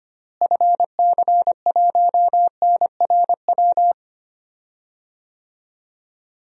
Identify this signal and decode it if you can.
Morse code, heard as a keyed tone, about 25 wpm, 700 Hz. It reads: FC1NRW
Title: DXCC-100 Call Signs - 6 Characters - Repeated 3x - 25wpm